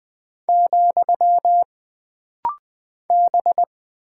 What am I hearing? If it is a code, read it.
Morse code, heard as a keyed tone, about 20 wpm, 700 Hz. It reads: , B
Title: Single Character (Letter:Number:Punctuation:Prosign) - Rapid-Fire - 20wpm